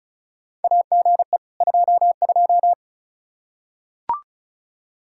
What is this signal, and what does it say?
Morse code, heard as a keyed tone, about 36 wpm, 700 Hz. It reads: AGE 22
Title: Practice QSO - Exchange 3 - More Info - 35wpm